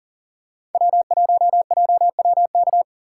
Morse code, heard as a keyed tone, about 40 wpm, 700 Hz. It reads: W1JWK
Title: USA Callsign DE Callsign - 5 Characters - 40wpm